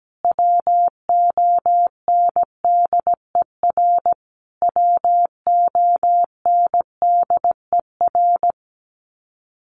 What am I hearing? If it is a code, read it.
Morse code, heard as a keyed tone, about 17 wpm, 700 Hz. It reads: WONDER WONDER
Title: Top 500 Words Repeated 17wpm